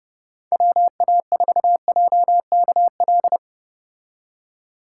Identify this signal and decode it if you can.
Morse code, heard as a keyed tone, about 30 wpm, 700 Hz. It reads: WA4JKL
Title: K1USN SST QSO Practice - 30wpm